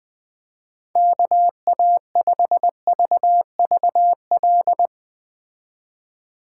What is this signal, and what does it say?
Morse code, heard as a keyed tone, about 20 wpm, 700 Hz. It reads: KA5VVL
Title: Speed Racing - USA Call Signs 17wpm